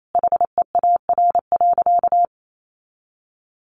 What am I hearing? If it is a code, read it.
Morse code, heard as a keyed tone, about 28 wpm, 700 Hz. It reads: HEAR.
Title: A Sound Of Thunder 28wpm